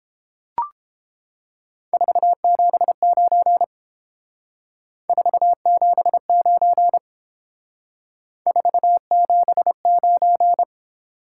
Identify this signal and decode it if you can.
Morse code, heard as a keyed tone, about 30 wpm, 700 Hz. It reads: 479 479 479